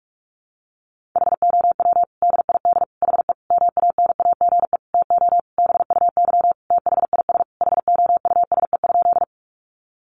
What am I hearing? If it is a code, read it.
Morse code, heard as a keyed tone, about 44 wpm, 700 Hz. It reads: HOW DID HE MANAGE TO BUY THIS HOUSE?
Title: Sentences from Top 1400 Words 45wpm